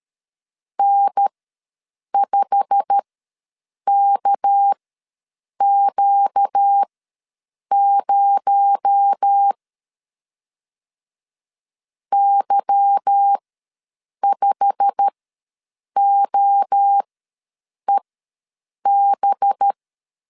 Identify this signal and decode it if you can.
Morse code, heard as a keyed tone, about 13 wpm, spaced as for 7 wpm, 800 Hz. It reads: N5KQ0 Y5OEB
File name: LEZIONI / 8 YJ45 / 8 YJ45 ws 12wpm 3146